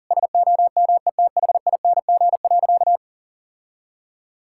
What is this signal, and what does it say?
Morse code, heard as a keyed tone, about 40 wpm, 700 Hz. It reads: SOMETHING.
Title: Sentences from Top 1600 Words 40wpm